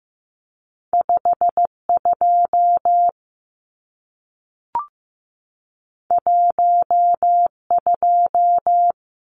Morse code, heard as a keyed tone, about 15 wpm, 700 Hz. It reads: 52 12
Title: Numbers - 2-Digit 15wpm